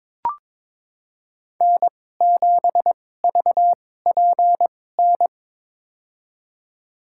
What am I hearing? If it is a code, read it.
Morse code, heard as a keyed tone, about 22 wpm, spaced as for 16 wpm, 700 Hz. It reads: N7VPN
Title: USA Call Signs - 5 Characters - 2x Character Spacing - 22wpm